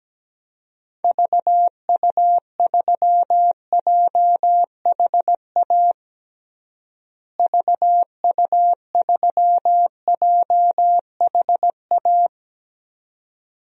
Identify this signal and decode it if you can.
Morse code, heard as a keyed tone, about 17 wpm, 700 Hz. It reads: VU3JHA VU3JHA
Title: DXCC-100 Call Signs - 6 Characters - Repeated 2x - 17wpm